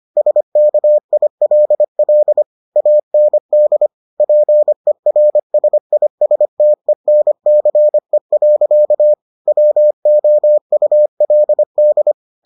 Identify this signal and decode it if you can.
Morse code, heard as a keyed tone, about 25 wpm, 600 Hz. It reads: SKILL AND PERSISTENCE. WOULD